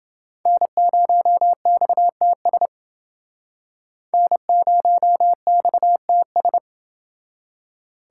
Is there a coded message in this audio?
Yes: Morse code from 0.5 to 6.6 s, heard as a keyed tone, about 30 wpm, 700 Hz. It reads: N0XTS N0XTS